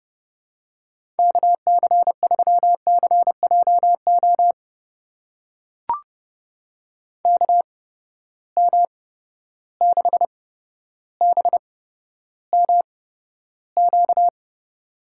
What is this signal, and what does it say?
Morse code, heard as a keyed tone, about 30 wpm, spaced as for 13 wpm, 700 Hz. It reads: KC3CJO KM6BMQ
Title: USA Call Signs - 6 Characters - 8x Character Spacing - 30wpm